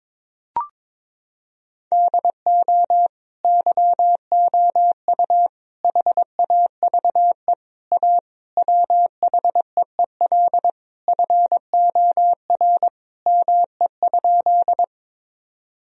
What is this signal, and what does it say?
Morse code, heard as a keyed tone, about 22 wpm, 700 Hz. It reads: DO YOU HAVE A WHEEL FOR ME?
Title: Sentences from Top 1000 Words - Review Part 3 - 22wpm